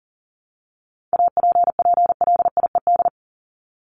Morse code, heard as a keyed tone, about 40 wpm, 700 Hz. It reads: APPLIED
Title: ED Suffix - Top 100 Words with Base Words 40wpm